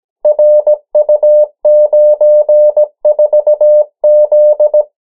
Morse code, heard as a keyed tone, about 17 wpm, 600 Hz. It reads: RU94Z
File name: morse mp3 / 14 urok 13